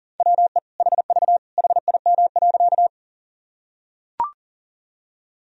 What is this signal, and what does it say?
Morse code, heard as a keyed tone, about 40 wpm, 700 Hz. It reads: WE HV HIM.
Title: Sentences from Top 100 Words and Abbreviations 40wpm